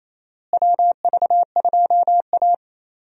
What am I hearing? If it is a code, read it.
Morse code, heard as a keyed tone, about 28 wpm, 700 Hz. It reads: WV2A
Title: USA Call Signs - 4 Characters - Repeated 2x - 28wpm